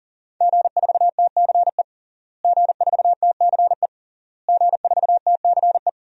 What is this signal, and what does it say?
Morse code, heard as a keyed tone, about 40 wpm, 700 Hz. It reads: G4TCE G4TCE G4TCE
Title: DXCC-100 Call Signs - 5 Characters - Repeated 3x - 40wpm